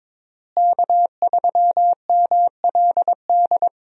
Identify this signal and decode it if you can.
Morse code, heard as a keyed tone, about 22 wpm, 700 Hz. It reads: K3MLD
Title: USA Call Signs - 5 Characters - Repeated 2x - 22wpm